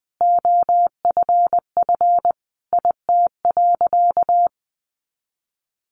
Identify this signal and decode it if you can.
Morse code, heard as a keyed tone, about 20 wpm, 700 Hz. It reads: OFF IT.
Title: A Sound Of Thunder 20wpm